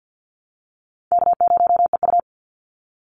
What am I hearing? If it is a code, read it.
Morse code, heard as a keyed tone, about 50 wpm, 700 Hz. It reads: K0EU
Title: Easy Call Signs Repeated 2x - 50wpm